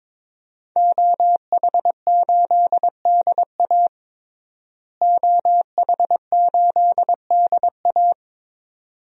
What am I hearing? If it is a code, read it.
Morse code, heard as a keyed tone, about 21 wpm, 700 Hz. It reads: OH8DA OH8DA
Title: DXCC-100 Call Signs - 5 Characters - Repeated 2x - 22wpm